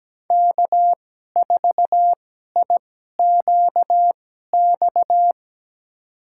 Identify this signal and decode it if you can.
Morse code, heard as a keyed tone, about 17 wpm, spaced as for 12 wpm, 700 Hz. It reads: K4IQX
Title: USA Call Signs - 5 Characters - 2x Character Spacing - 17wpm